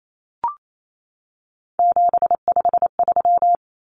Morse code, heard as a keyed tone, about 28 wpm, 700 Hz. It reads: 753
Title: Common Cut Numbers - 3-Digit 28wpm